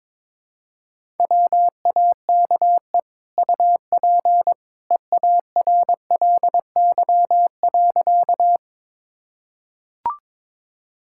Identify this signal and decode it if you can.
Morse code, heard as a keyed tone, about 22 wpm, 700 Hz. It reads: WAKE UP EARLY.
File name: practice / 22wpm / Easy Sentences with Hints - 22wpm